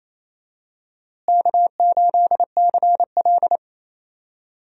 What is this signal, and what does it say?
Morse code, heard as a keyed tone, about 28 wpm, 700 Hz. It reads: K8CL
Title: USA Call Signs - 4 Characters - Repeated 2x - 28wpm